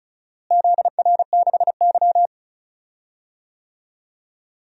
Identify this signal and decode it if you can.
Morse code, heard as a keyed tone, about 36 wpm, 700 Hz. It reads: ZR6Y
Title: DXCC-100 Call Signs - 4 Characters - Repeated 2x - 35wpm